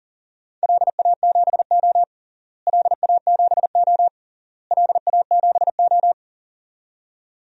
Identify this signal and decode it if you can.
Morse code, heard as a keyed tone, about 40 wpm, 700 Hz. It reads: LA7O LA7O LA7O
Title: DXCC-100 Call Signs - 4 Characters - Repeated 3x - 40wpm